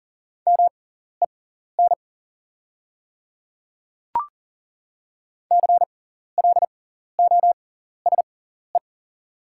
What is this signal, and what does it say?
Morse code, heard as a keyed tone, about 40 wpm, spaced as for 14 wpm, 700 Hz. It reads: MEN CLOSE
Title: 6x Character Spacing - Top 300 Words - 40wpm